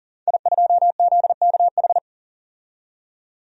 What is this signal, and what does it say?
Morse code, heard as a keyed tone, about 39 wpm, 700 Hz. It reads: I2ZKH
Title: DXCC-100 Call Signs - 5 Characters - 40wpm